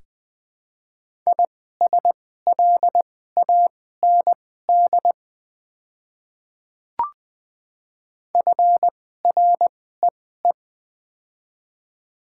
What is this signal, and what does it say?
Morse code, heard as a keyed tone, about 20 wpm, spaced as for 15 wpm, 690 Hz. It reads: ISLAND FREE
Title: 2x Character Spacing - Top 500 Words - 20wpm